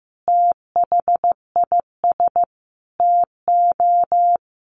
Morse code, heard as a keyed tone, about 15 wpm, 700 Hz. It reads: THIS TO
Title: Sentences from Top 1200 Words 15wpm